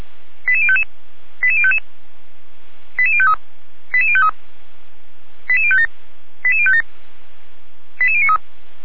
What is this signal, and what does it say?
Five-tone selective call (ZVEI), heard as quick runs of five tones, tones of about 70 ms. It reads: 88055 88055 88053 88053 88057 88057 88093